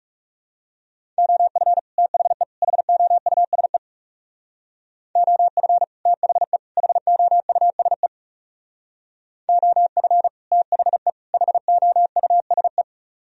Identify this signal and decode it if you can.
Morse code, heard as a keyed tone, about 36 wpm, 700 Hz. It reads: OF THE HOUSE OF THE HOUSE OF THE HOUSE
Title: Speed-Racing - Sets of 3 Words 30wpm